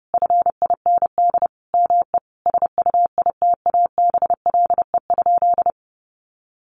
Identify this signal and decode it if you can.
Morse code, heard as a keyed tone, about 30 wpm, 700 Hz. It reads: FIND ME SUITABLE?
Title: Sentences from Top 1400 Words 30wpm